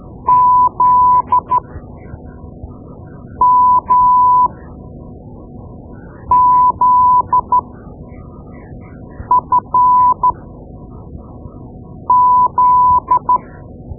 Morse code, heard as a keyed tone, about 12 wpm, spaced as for 4 wpm, 1000 Hz. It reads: ZMZFZ
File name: Morse Lessons / Morse Lessons / Tape-2B